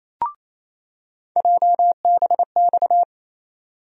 Morse code, heard as a keyed tone, about 27 wpm, 700 Hz. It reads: JBX
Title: USA Call Sign Postfixes - 3 Letters 28wpm